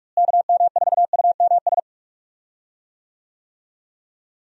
Morse code, heard as a keyed tone, about 46 wpm, 700 Hz. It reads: KM4UMS
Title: USA Call Signs - 6 Characters - Repeated 2x - 45wpm